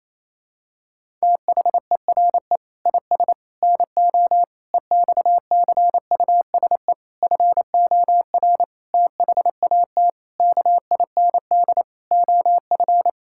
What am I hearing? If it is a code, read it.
Morse code, heard as a keyed tone, about 27 wpm, 700 Hz. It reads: THERE IS NO EXCUSE FOR THAT KIND OF
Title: Sentences from Top 1000 Words - Review Part 6 - 28wpm